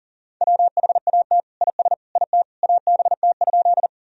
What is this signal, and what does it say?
Morse code, heard as a keyed tone, about 40 wpm, 700 Hz. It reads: WHAT IS IT ABT?
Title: Sentences from Top 100 Words and Abbreviations 40wpm